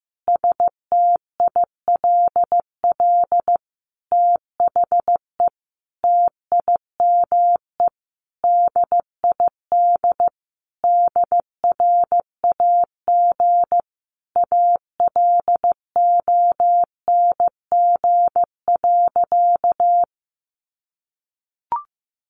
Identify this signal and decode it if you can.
Morse code, heard as a keyed tone, about 15 wpm, 700 Hz. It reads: STILL THE TIME DID DRAG ALONG.
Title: Sentences from Top 1500 Words 15wpm